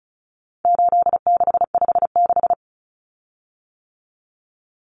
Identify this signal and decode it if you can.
Morse code, heard as a keyed tone, about 34 wpm, 700 Hz. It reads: 8656
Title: Common Cut Numbers - 4-Digit 35wpm